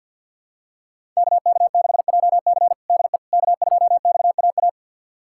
Morse code, heard as a keyed tone, about 51 wpm, 700 Hz. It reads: KK6JC DE K2XAA
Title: USA Callsign DE Callsign - 5 Characters - 50wpm